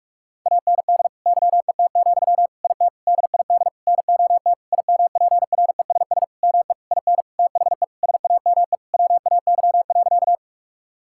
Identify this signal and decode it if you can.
Morse code, heard as a keyed tone, about 46 wpm, 700 Hz. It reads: AND YET, IT DID NOT IMPRESS ME IN THE SAME WAY.
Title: Sentences from Top 1600 Words 45wpm